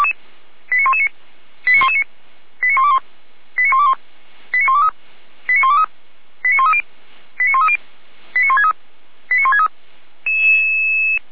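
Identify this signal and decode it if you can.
Five-tone selective call (ZVEI), heard as quick runs of five tones, tones of about 70 ms. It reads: 89119 89119 89121 89121 89123 89123 89130 89130 89163 89163
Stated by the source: ZVEI five-tone selective call received by SDR, from the Feuerwehr Orlamünde fire brigade audio archive.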